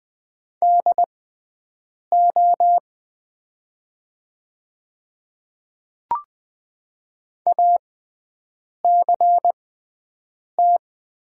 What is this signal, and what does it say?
Morse code, heard as a keyed tone, about 20 wpm, spaced as for 7 wpm, 700 Hz. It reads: DO ACT
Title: 6x Character Spacing - Top 500 Words - 20wpm